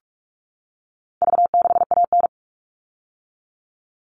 Morse code, heard as a keyed tone, about 44 wpm, 700 Hz. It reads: V6AN
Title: All Cut Numbers - 4-Digit 45wpm